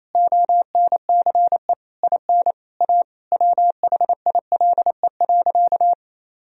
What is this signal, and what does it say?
Morse code, heard as a keyed tone, about 28 wpm, 700 Hz. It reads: ONCE IN A WHILE.
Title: Sentences from Top 500 Words and Abbreviations 28wpm